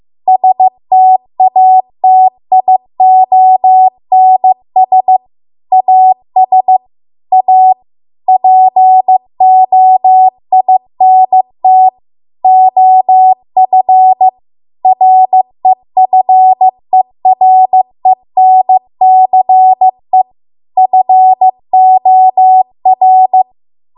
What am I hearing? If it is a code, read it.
Morse code, heard as a keyed tone, about 15 wpm, 750 Hz. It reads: STATIONS AS A POINT OF REFERENCE FOR
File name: Archive / 15 WPM / 15WPM